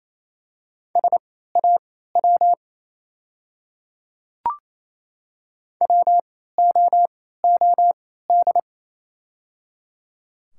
Morse code, heard as a keyed tone, about 27 wpm, spaced as for 16 wpm, 690 Hz. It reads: SAW WOOD
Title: 3x Character Spacing - Top 300 Words - 28wpm